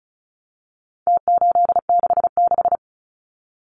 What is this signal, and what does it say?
Morse code, heard as a keyed tone, about 34 wpm, 700 Hz. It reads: T866
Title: Numbers - Contest Style - 4-Digit 35wpm